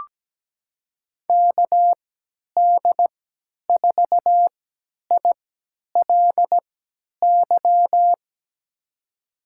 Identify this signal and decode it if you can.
Morse code, heard as a keyed tone, about 17 wpm, spaced as for 10 wpm, 700 Hz. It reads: KD4ILY
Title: USA Call Signs - 6 Characters - 3x Character Spacing - 17wpm